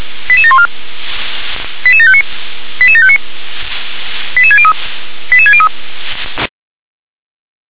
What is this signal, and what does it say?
Five-tone selective call (ZVEI), heard as quick runs of five tones, tones of about 70 ms. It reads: 88714 88759 88759 88603 88603